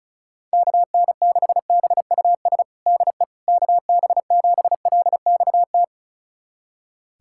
Morse code, heard as a keyed tone, about 36 wpm, 700 Hz. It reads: KN6BUS DE KB7LXT
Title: USA Callsign DE Callsign - 6 Characters - 35wpm